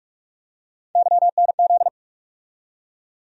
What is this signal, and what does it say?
Morse code, heard as a keyed tone, about 46 wpm, 700 Hz. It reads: YNZ
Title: USA Call Sign Postfixes - 3 Letters 45wpm